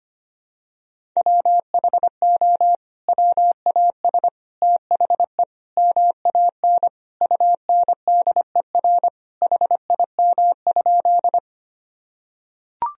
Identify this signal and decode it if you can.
Morse code, heard as a keyed tone, about 25 wpm, 700 Hz. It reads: WHO WAS THE MAN UNDER HIM?
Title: Sentences from Top 1000 Words - Review Part 1 - 25wpm